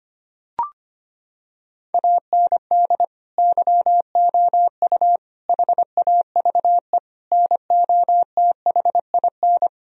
Morse code, heard as a keyed tone, about 25 wpm, 700 Hz. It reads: AND YOU HAVE NOTHIN
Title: Sentences from Top 1000 Words - Review Part 1 - 25wpm